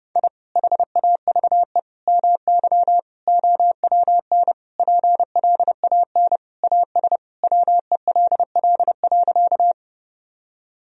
Morse code, heard as a keyed tone, about 30 wpm, 700 Hz. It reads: I HAVE MY OWN PLAN AS WELL.
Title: Sentences from Top 1000 Words - Review Part 2 - 30wpm